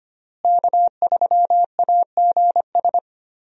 Morse code, heard as a keyed tone, about 25 wpm, 700 Hz. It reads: K3AGS
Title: USA Call Signs - 5 Characters - Repeated 3x - 25wpm